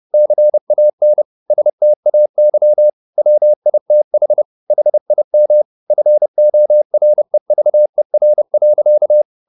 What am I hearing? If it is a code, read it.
Morse code, heard as a keyed tone, about 30 wpm, 600 Hz. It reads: CAN STAY WITH HIM FOREVER.
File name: the old man and the sea / the old man and the sea 1 30wpm